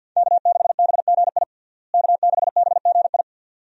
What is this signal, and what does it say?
Morse code, heard as a keyed tone, about 52 wpm, 700 Hz. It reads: K6BGI K6BGI
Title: USA Call Signs - 5 Characters - Repeated 2x - 50wpm